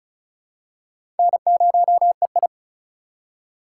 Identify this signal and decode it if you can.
Morse code, heard as a keyed tone, about 34 wpm, 700 Hz. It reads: N0EI
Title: USA Call Signs - 4 Characters - 2x Character Spacing - 35wpm